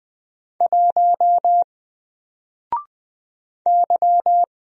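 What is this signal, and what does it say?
Morse code, heard as a keyed tone, about 20 wpm, 700 Hz. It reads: 1 Y
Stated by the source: Single letter and number Morse drill at 20 wpm, from N4LCD.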